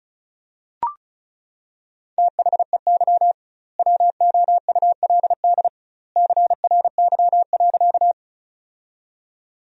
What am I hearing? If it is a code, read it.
Morse code, heard as a keyed tone, about 34 wpm, 700 Hz. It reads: THEY WOULD CRY.
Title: 2x Word-Spacing - Sentences from Top 500 Words - 35wpm